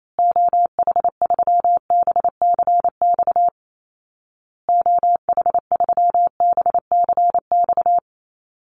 Morse code, heard as a keyed tone, about 28 wpm, 700 Hz. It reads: OH3BCX OH3BCX
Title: Call Signs Repeated 2x - 28wpm